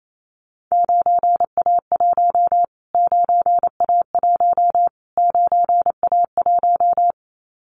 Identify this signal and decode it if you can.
Morse code, heard as a keyed tone, about 28 wpm, 700 Hz. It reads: 9A1 9A1 9A1
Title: Callsign Prefixes for Top 25 DXCC - Repeated 3x - 28wpm